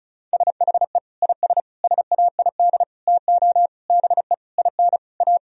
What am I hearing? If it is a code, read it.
Morse code, heard as a keyed tone, about 34 wpm, 700 Hz. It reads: SHE IS SAID TO BE IN A
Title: Sentences from Top 1000 Words - Review Part 2 - 35wpm